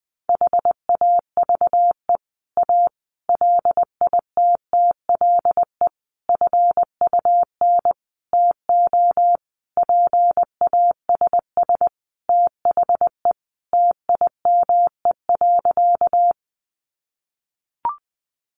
Morse code, heard as a keyed tone, about 19 wpm, 700 Hz. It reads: HAVE A LITTLE FUN TO PASS THE TIME.
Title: Sentences from Top 700 Words 20wpm